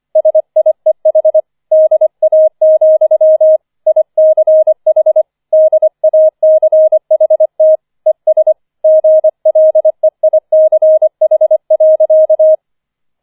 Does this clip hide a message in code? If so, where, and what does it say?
Yes: Morse code from 0.2 to 12.6 s, heard as a keyed tone, about 25 wpm, 630 Hz. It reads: SIEH DA, ICH DACHT ES GLEICH.